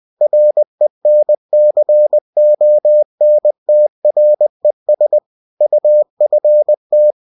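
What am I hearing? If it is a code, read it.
Morse code, heard as a keyed tone, about 20 wpm, 600 Hz. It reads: RENCONTRES UFT
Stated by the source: Bulletin sent in CW.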